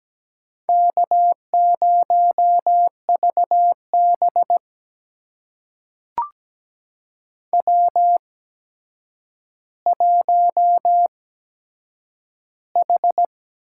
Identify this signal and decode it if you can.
Morse code, heard as a keyed tone, about 17 wpm, spaced as for 8 wpm, 700 Hz. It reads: K0VB W1H
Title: USA Call Signs - 4 Characters - 8x Character Spacing - 17wpm